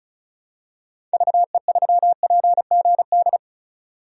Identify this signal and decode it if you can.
Morse code, heard as a keyed tone, about 36 wpm, 700 Hz. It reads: VE3PGD
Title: DXCC-100 Call Signs - 6 Characters - 35wpm